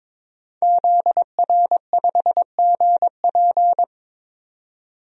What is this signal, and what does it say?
Morse code, heard as a keyed tone, about 22 wpm, 700 Hz. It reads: ZR5GP
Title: DXCC-100 Call Signs - 5 Characters - 22wpm